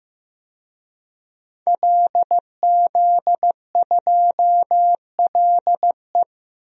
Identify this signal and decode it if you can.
Morse code, heard as a keyed tone, about 15 wpm, 700 Hz. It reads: LZ2LE
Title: DXCC-100 Call Signs - 5 Characters - Repeated 2x - 15wpm